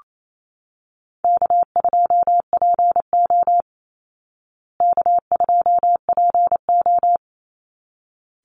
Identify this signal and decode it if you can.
Morse code, heard as a keyed tone, about 28 wpm, 700 Hz. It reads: K2PO K2PO
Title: Call Signs Repeated 3x - 28wpm